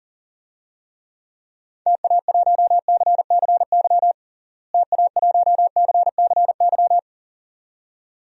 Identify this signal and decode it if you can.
Morse code, heard as a keyed tone, about 37 wpm, 700 Hz. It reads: TA1CCY TA1CCY
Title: DXCC-100 Call Signs - 6 Characters - Repeated 2x - 40wpm